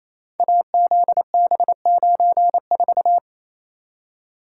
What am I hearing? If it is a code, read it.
Morse code, heard as a keyed tone, about 28 wpm, 700 Hz. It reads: AZB94
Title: Code Groups - 5 Letter:Numbers - 28wpm